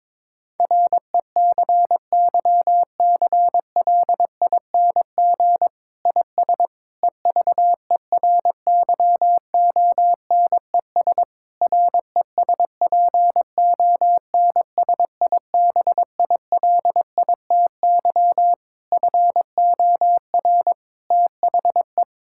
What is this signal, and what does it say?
Morse code, heard as a keyed tone, about 22 wpm, 700 Hz. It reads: RECYCLING IS EVERYONES RESPONSIBILITY FOR THE
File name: Difficult Sentences with Hints - 22wpm